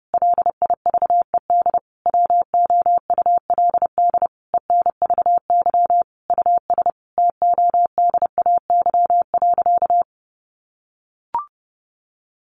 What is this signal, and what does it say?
Morse code, heard as a keyed tone, about 30 wpm, 700 Hz. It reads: LIVED WOULD ENVY US TODAY.
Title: A Sound Of Thunder 30wpm